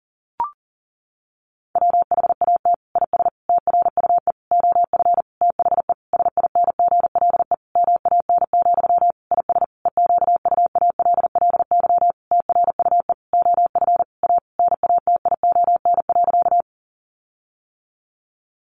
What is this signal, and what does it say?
Morse code, heard as a keyed tone, about 40 wpm, 700 Hz. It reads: WHAT IS TRUE OF THE SINGLE MAN, IS EQUALLY TRUE OF A NATION.
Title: Sentences from Top 1400 Words 40wpm